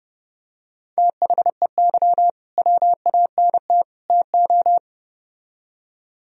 Morse code, heard as a keyed tone, about 30 wpm, 700 Hz. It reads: THEY WANT TO